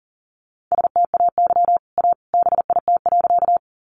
Morse code, heard as a keyed tone, about 40 wpm, 700 Hz. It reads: STAY A BIT.
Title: Sentences from Top 900 Words 40wpm